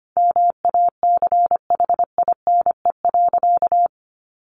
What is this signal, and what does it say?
Morse code, heard as a keyed tone, about 25 wpm, 700 Hz. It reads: MACHINE.
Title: A Sound Of Thunder 25wpm